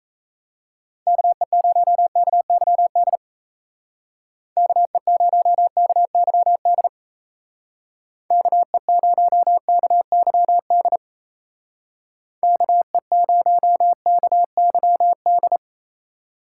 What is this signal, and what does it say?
Morse code, heard as a keyed tone, about 34 wpm, 700 Hz. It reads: KE0KYD KE0KYD KE0KYD KE0KYD
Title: Speed Racing - USA 2x3 Call Signs 28wpm